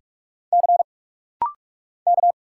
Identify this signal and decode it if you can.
Morse code, heard as a keyed tone, about 46 wpm, 700 Hz. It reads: C K
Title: Single Letter-Number - Rapid-Fire 45wpm